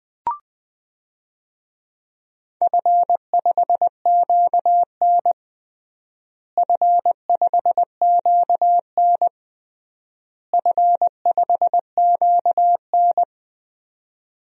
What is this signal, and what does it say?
Morse code, heard as a keyed tone, about 20 wpm, 700 Hz. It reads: F5QN F5QN F5QN
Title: DXCC-100 Call Signs - 4 Characters - Repeated 3x - 20wpm